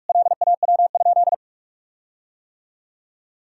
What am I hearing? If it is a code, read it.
Morse code, heard as a keyed tone, about 46 wpm, 700 Hz. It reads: LAW?
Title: Sentences from Top 1600 Words 45wpm